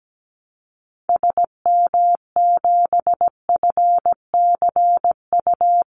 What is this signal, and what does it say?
Morse code, heard as a keyed tone, about 17 wpm, 700 Hz. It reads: SM7FCU
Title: Call Signs Repeated 2x - 17wpm